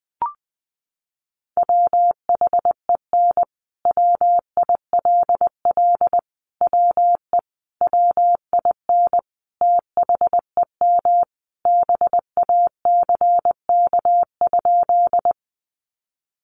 Morse code, heard as a keyed tone, about 20 wpm, 700 Hz. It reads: WHEN WILL WE WIN THEM BACK?
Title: Sentences from Top 1500 Words 20wpm